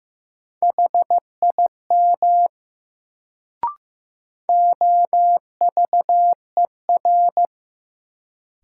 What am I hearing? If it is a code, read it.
Morse code, heard as a keyed tone, about 15 wpm, 700 Hz. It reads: HIM OVER
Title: Sentences from Top 100 Words - New Word Drill - 15wpm